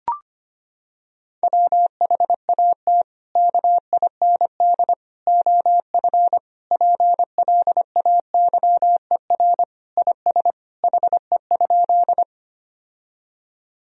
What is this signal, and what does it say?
Morse code, heard as a keyed tone, about 25 wpm, 700 Hz. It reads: WHAT KIND OF PLAYER IS HE?
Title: Sentences from Top 1000 Words - Review Part 4 - 25wpm